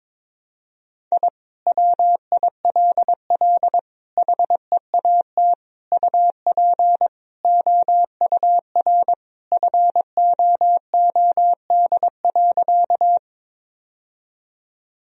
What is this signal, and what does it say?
Morse code, heard as a keyed tone, about 22 wpm, 700 Hz. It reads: I WILL HEAT UP OUR FOOD.
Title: Sentences from Top 1000 Words - Review Part 3 - 22wpm